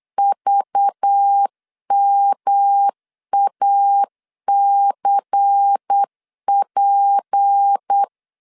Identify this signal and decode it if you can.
Morse code, heard as a keyed tone, about 8 wpm, 800 Hz. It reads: VMACP
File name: LEZIONI / 6 PKX0 / 6 P 8wpm 3025